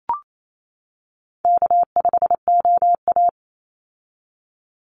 Morse code, heard as a keyed tone, about 28 wpm, 700 Hz. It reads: K5OA
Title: Call Signs - 28wpm